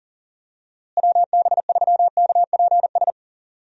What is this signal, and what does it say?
Morse code, heard as a keyed tone, about 40 wpm, 700 Hz. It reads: WB3KPS
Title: USA Call Signs - 6 Characters - Repeated 2x - 40wpm